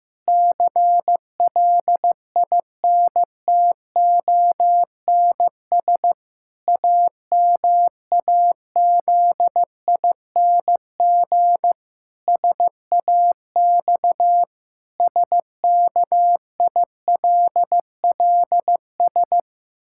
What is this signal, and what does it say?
Morse code, heard as a keyed tone, about 15 wpm, 700 Hz. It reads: CLINTONS AMAZING SAX SKILLS
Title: Pangram Challenge v2 - 15wpm